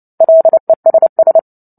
Morse code, heard as a keyed tone, about 30 wpm, 650 Hz. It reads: LESS
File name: Book Two / 03-Chapter 3 - 29 WPM